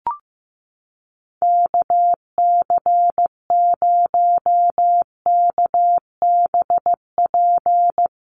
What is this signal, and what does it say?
Morse code, heard as a keyed tone, about 15 wpm, 700 Hz. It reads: KC0KBP
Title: USA Call Signs Repeated 3x 15wpm